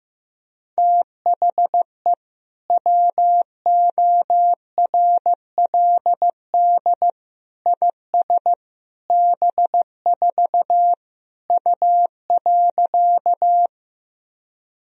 Morse code, heard as a keyed tone, about 15 wpm, 700 Hz. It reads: THE WORLD IS B4 U.
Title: Sentences from Top 200 Words and Abbreviations 15wpm